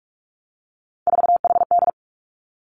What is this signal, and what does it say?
Morse code, heard as a keyed tone, about 46 wpm, 700 Hz. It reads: 4HD
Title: Three Challenging Letter-Number Combinations 45wpm